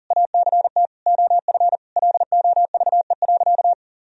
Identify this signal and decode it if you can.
Morse code, heard as a keyed tone, about 40 wpm, 700 Hz. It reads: ACT OF LOVE.
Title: Sentences from Top 1000 Words - Review Part 3 - 40wpm